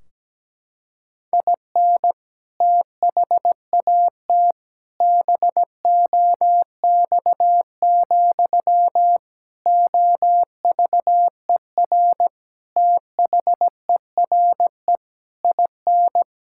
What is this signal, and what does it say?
Morse code, heard as a keyed tone, about 17 wpm, 700 Hz. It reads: IN THAT BOX, OVER THERE IN
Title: Sentences from Top 1600 Words 17wpm